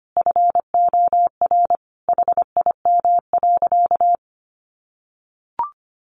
Morse code, heard as a keyed tone, about 25 wpm, 700 Hz. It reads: FOR HIM.
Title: Sentences from Top 1100 Words 25wpm